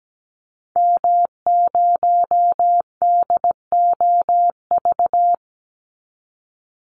Morse code, heard as a keyed tone, about 17 wpm, 700 Hz. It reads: M0DOV
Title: Call Signs Repeated 2x - 17wpm